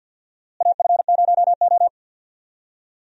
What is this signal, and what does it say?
Morse code, heard as a keyed tone, about 49 wpm, 700 Hz. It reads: AF0O
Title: USA Call Signs - 4 Characters - Repeated 3x - 50wpm